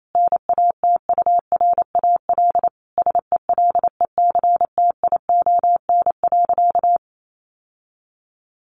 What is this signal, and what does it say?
Morse code, heard as a keyed tone, about 28 wpm, 700 Hz. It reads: NATURAL SELECTION.
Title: Sentences from Top 1400 Words 28wpm